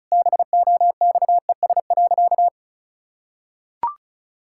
Morse code, heard as a keyed tone, about 36 wpm, 700 Hz. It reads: BOXES.
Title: Pangram Challenge v2 - 35wpm